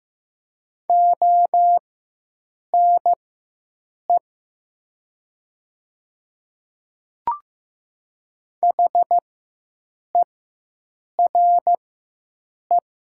Morse code, heard as a keyed tone, about 15 wpm, spaced as for 7 wpm, 700 Hz. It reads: ONE HERE
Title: 4x Character Spacing - Top 200 Words - 15wpm